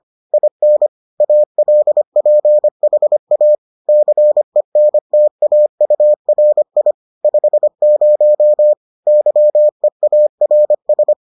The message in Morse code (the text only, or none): IN ALPHA CENTAURI 50 YEARS